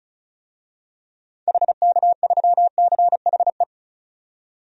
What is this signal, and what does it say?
Morse code, heard as a keyed tone, about 36 wpm, 700 Hz. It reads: HK3CHE
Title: DXCC-100 Call Signs - 6 Characters - 35wpm